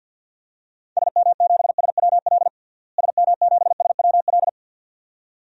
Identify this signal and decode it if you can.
Morse code, heard as a keyed tone, about 51 wpm, 700 Hz. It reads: SM7SWL SM7SWL
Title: DXCC-100 Call Signs - 6 Characters - Repeated 2x - 50wpm